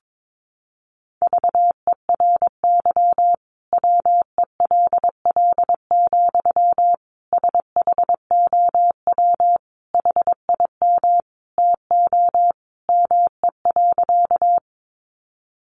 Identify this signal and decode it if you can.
Morse code, heard as a keyed tone, about 22 wpm, 700 Hz. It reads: VERY WELL, SHOW HIM TO ME.
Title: Sentences from Top 200 Words 22wpm